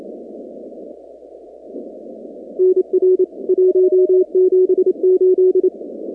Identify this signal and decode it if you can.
Morse code, heard as a keyed tone, about 28 wpm, 370 Hz. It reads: NR 178
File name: contests / 2019-cw-wpx-kt5j-k5pi- -2330 0-45